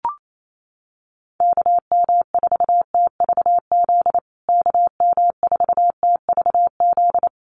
Morse code, heard as a keyed tone, about 28 wpm, 700 Hz. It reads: KM4TVZ KM4TVZ
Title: USA Call Signs Repeated 3x 28wpm